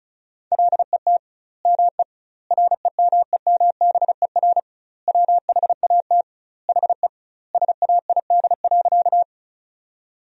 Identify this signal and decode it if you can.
Morse code, heard as a keyed tone, about 36 wpm, 700 Hz. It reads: LET ME REMEMBER WHAT HE SAID.
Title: 2x Word-Spacing - Sentences from Top 400 Words - 35wpm